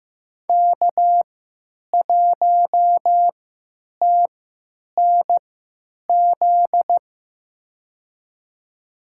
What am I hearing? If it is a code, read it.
Morse code, heard as a keyed tone, about 15 wpm, spaced as for 8 wpm, 700 Hz. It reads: K1TNZ
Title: USA Call Signs - 5 Characters - 3x Character Spacing - 15wpm